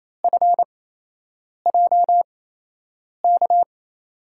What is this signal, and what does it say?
Morse code, heard as a keyed tone, about 28 wpm, spaced as for 8 wpm, 700 Hz. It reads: FJK
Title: USA Call Signs - 6 Characters - 8x Character Spacing - 28wpm